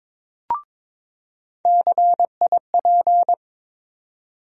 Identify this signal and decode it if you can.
Morse code, heard as a keyed tone, about 22 wpm, 700 Hz. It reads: CIP
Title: USA Call Sign Postfixes - 3 Letters 22wpm